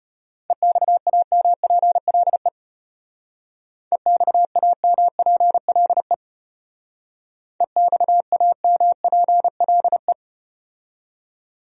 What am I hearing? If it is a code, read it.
Morse code, heard as a keyed tone, about 34 wpm, 700 Hz. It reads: EXAMPLE EXAMPLE EXAMPLE